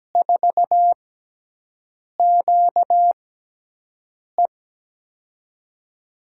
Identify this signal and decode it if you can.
Morse code, heard as a keyed tone, about 17 wpm, 700 Hz. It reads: 4 Q E
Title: USA Call Signs - 5 Characters - 6x Character Spacing - 17wpm